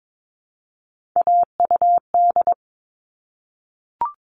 Morse code, heard as a keyed tone, about 22 wpm, 700 Hz. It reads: AUD
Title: All Cut Numbers - 3-Digit 22wpm